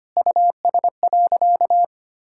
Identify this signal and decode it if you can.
Morse code, heard as a keyed tone, about 25 wpm, 700 Hz. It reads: US.
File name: practice / 25wpm / Sentences with 2-Word Phrases - 25wpm